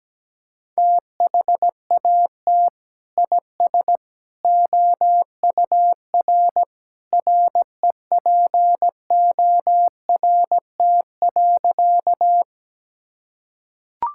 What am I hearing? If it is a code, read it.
Morse code, heard as a keyed tone, about 17 wpm, 700 Hz. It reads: THAT IS OUR REPORT.
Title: Sentences from Top 1000 Words - Review Part 6 - 17wpm